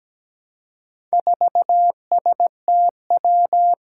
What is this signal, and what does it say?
Morse code, heard as a keyed tone, about 17 wpm, 700 Hz. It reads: 4STW
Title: Four Challenging Letter-Number Combinations 17wpm